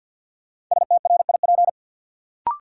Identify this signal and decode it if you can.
Morse code, heard as a keyed tone, about 52 wpm, 700 Hz. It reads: STRIP
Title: Sentences from Top 1600 Words - New Word Drill - 50wpm